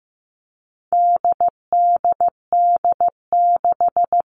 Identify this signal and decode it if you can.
Morse code, heard as a keyed tone, about 15 wpm, 700 Hz. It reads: DDD6
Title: All Cut Numbers - 4-Digit 15wpm